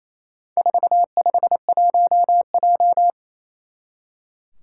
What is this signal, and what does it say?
Morse code, heard as a keyed tone, about 28 wpm, 700 Hz. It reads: 451J
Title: Four Challenging Letter-Number Combinations 28wpm